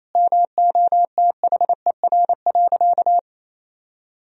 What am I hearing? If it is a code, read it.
Morse code, heard as a keyed tone, about 28 wpm, 700 Hz. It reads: MOTHER.
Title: Sentences from Top 1000 Words - Review Part 3 - 28wpm